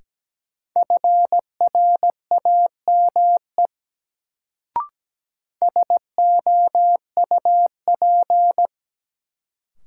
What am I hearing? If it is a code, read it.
Morse code, heard as a keyed tone, about 17 wpm, 700 Hz. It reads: FRAME SOUP
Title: Sentences from Top 1000 Words - Review Part 6 - New Word Drill - 17wpm